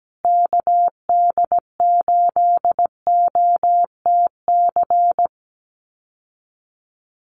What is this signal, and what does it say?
Morse code, heard as a keyed tone, about 17 wpm, 700 Hz. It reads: KD8OTC
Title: USA Call Signs Repeated 3x 17wpm